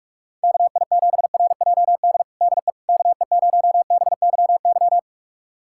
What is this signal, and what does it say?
Morse code, heard as a keyed tone, about 47 wpm, 700 Hz. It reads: KI7RJD DE KE0BYY
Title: USA Callsign DE Callsign - 6 Characters - 45wpm